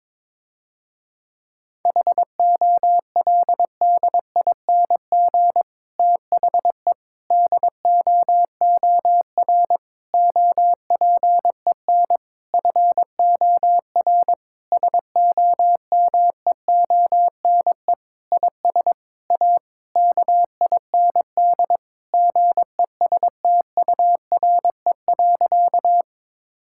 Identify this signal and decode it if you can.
Morse code, heard as a keyed tone, about 22 wpm, 700 Hz. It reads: HOLDING THE DOOR OPEN FOR SOMEONE IS A KIND GESTURE.
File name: Difficult Sentences with Hints - 22wpm